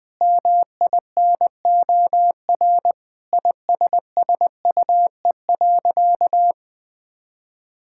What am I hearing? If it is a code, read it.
Morse code, heard as a keyed tone, about 20 wpm, 700 Hz. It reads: MINOR ISSUE.
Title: Sentences with 6-Word Phrases - 20wpm